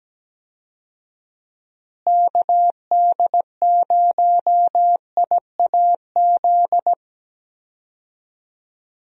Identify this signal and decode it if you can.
Morse code, heard as a keyed tone, about 17 wpm, 700 Hz. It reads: KD0IAZ
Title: USA Call Signs - 6 Characters - Repeated 2x - 17wpm